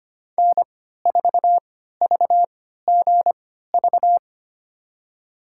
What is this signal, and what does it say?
Morse code, heard as a keyed tone, about 25 wpm, spaced as for 14 wpm, 700 Hz. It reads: N4VGV
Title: USA Call Signs - 5 Characters - 3x Character Spacing - 25wpm